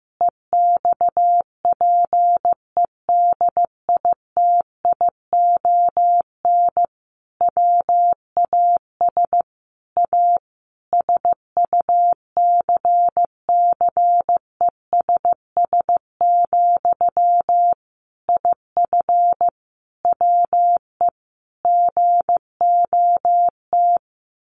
Morse code, heard as a keyed tone, about 15 wpm, 700 Hz. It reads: EXPEDITION WAS A SUCCESS, IF WE GOT
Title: A Sound Of Thunder 15wpm